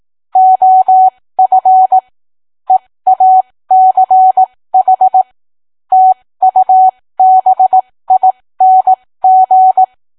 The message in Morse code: OF EACH TUBING